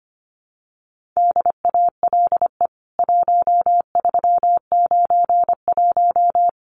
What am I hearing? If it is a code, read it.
Morse code, heard as a keyed tone, about 25 wpm, 700 Hz. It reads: DALE 1391
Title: CWT Exchange 25wpm